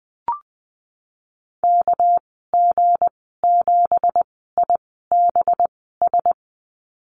Morse code, heard as a keyed tone, about 20 wpm, spaced as for 15 wpm, 700 Hz. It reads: KG7IBS
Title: 200 Lesson 32g - Letter k - Call Signs 20wpm